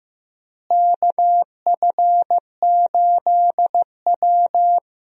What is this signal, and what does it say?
Morse code, heard as a keyed tone, about 15 wpm, 700 Hz. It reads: KF8W
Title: USA Call Signs - 4 Characters - Repeated 3x - 15wpm